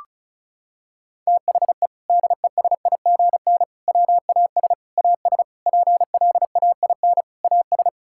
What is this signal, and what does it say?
Morse code, heard as a keyed tone, about 36 wpm, 700 Hz. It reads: THE DESIGN WAS AS PLAIN AS
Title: Sentences from Top 1000 Words - Review Part 5 - 35wpm